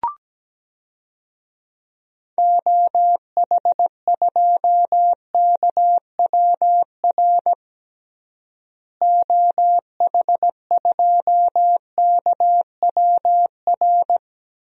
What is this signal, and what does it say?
Morse code, heard as a keyed tone, about 17 wpm, 700 Hz. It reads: OH2KWR OH2KWR
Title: DXCC-100 Call Signs - 6 Characters - Repeated 2x - 17wpm